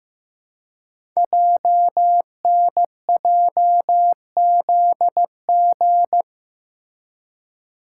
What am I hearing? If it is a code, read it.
Morse code, heard as a keyed tone, about 15 wpm, 700 Hz. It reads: JNJZG
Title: Code Groups - 5 Letters - 15wpm